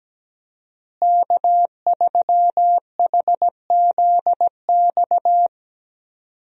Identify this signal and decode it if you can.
Morse code, heard as a keyed tone, about 17 wpm, 700 Hz. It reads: K3HZX
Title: USA Call Signs - 5 Characters - Repeated 2x - 17wpm